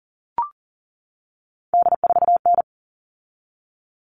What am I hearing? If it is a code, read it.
Morse code, heard as a keyed tone, about 40 wpm, 700 Hz. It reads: D4N
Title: All Cut Numbers - 3-Digit 40wpm